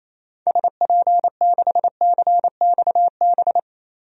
Morse code, heard as a keyed tone, about 28 wpm, 700 Hz. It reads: SP6CXB
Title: DXCC-100 Call Signs - 6 Characters - 28wpm